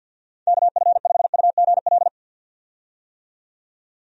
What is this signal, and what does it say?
Morse code, heard as a keyed tone, about 49 wpm, 700 Hz. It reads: KF5UGL
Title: USA Call Signs - 6 Characters - 50wpm